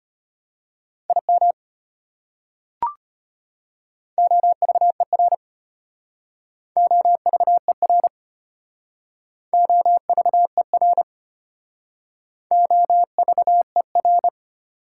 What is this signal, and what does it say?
Morse code, heard as a keyed tone, about 33 wpm, 690 Hz. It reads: IM OVER OVER OVER OVER